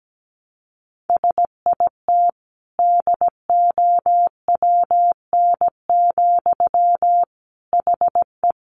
Morse code, heard as a keyed tone, about 17 wpm, 700 Hz. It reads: SIT DOWN, HE
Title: Sentences from Top 500 Words 17wpm